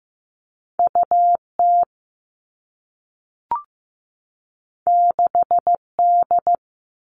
Morse code, heard as a keyed tone, about 15 wpm, 700 Hz. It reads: U T 6 D
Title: Two Challenging Letter-Number Combinations 15wpm